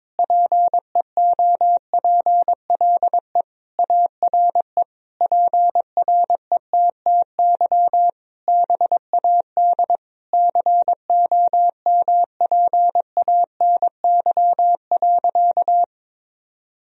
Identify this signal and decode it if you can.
Morse code, heard as a keyed tone, about 22 wpm, 700 Hz. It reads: PEOPLE ARE PRETTY BAD COMPANY.
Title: Sentences from Top 1000 Words - Review Part 3 - 22wpm